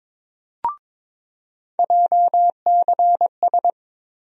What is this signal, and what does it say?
Morse code, heard as a keyed tone, about 22 wpm, 700 Hz. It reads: JCS
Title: USA Call Sign Postfixes - 3 Letters 22wpm